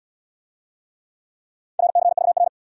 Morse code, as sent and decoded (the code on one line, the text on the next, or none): .-. -.-. ...- .-.
RCVR